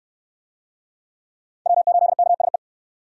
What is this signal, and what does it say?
Morse code, heard as a keyed tone, about 62 wpm, 700 Hz. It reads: W8KSE